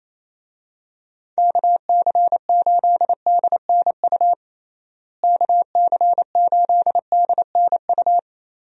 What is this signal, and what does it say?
Morse code, heard as a keyed tone, about 27 wpm, 700 Hz. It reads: KC8DNU KC8DNU
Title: USA Call Signs - 6 Characters - Repeated 3x - 28wpm